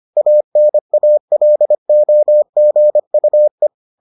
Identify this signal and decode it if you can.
Morse code, heard as a keyed tone, about 25 wpm, 600 Hz. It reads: ANALOGUE